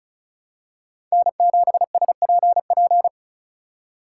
Morse code, heard as a keyed tone, about 36 wpm, 700 Hz. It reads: N7SPP
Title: USA Call Signs - 5 Characters - 3x Character Spacing - 35wpm